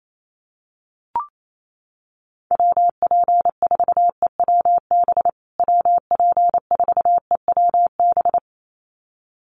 Morse code, heard as a keyed tone, about 28 wpm, 700 Hz. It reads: WP4EWB WP4EWB
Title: USA Call Signs Repeated 2x 28wpm